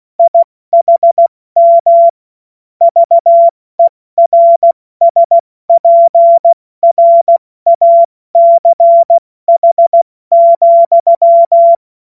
Morse code, heard as a keyed tone, about 16 wpm, 680 Hz. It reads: IHM VERSPRACH,
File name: Morsekurs / Witze / Witz 30. August 2022 70 F80